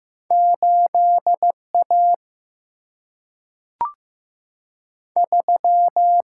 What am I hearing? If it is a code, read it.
Morse code, heard as a keyed tone, about 15 wpm, 700 Hz. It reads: 8A 3
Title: Single-Letter Single-Number 15wpm